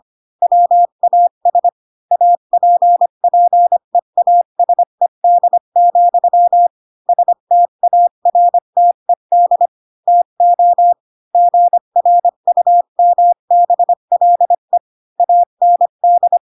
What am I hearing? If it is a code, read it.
Morse code, heard as a keyed tone, about 25 wpm, 700 Hz. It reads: WAS APPEASED, STARTED TO GRUMBLE AND
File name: Pinocchio / Pinocchio ch8 20 25 ik1hge